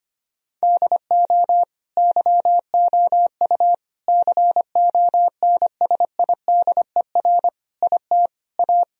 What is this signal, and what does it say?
Morse code, heard as a keyed tone, about 25 wpm, 700 Hz. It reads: DO YOU CONSIDER IT A
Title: Sentences from Top 1000 Words - Review Part 4 - 25wpm